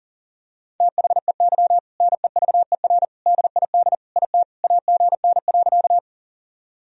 Morse code, heard as a keyed tone, about 40 wpm, 700 Hz. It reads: THEY NEVER DID IT AGN.
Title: Sentences from Top 300 Words and Abbreviations 40wpm